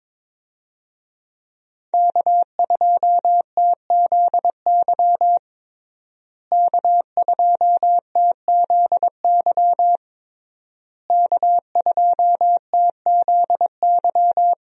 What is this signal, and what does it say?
Morse code, heard as a keyed tone, about 22 wpm, 700 Hz. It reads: K2TZY K2TZY K2TZY
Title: USA Call Signs - 5 Characters - Repeated 3x - 22wpm